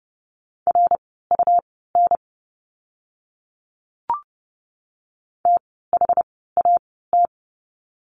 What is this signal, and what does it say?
Morse code, heard as a keyed tone, about 30 wpm, spaced as for 17 wpm, 700 Hz. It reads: RUN THAT
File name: course / 30:10wpm / 237 Lesson 38f - Letter q - Words 30wpm